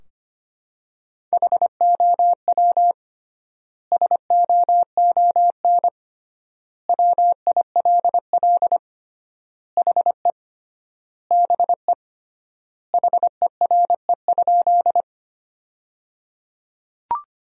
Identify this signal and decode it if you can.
Morse code, heard as a keyed tone, about 25 wpm, 700 Hz. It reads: HOW SOON WILL HE BE HERE?
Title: 3x Word-Spacing - Sentences from Top 400 Words - 25wpm